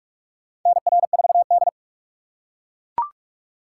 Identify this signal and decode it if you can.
Morse code, heard as a keyed tone, about 42 wpm, 700 Hz. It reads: NR4D
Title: USA Call Signs - 4 Characters - 4x Character Spacing - 45wpm